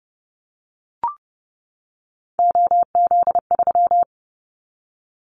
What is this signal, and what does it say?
Morse code, heard as a keyed tone, about 30 wpm, 700 Hz. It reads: OZ3
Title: Callsign Prefixes for Top 100 DXCC - 30wpm